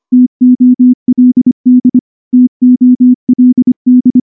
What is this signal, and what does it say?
Morse code, heard as a keyed tone, about 24 wpm, 250 Hz. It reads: TOLD TOLD
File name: morse / morse-training-25WPM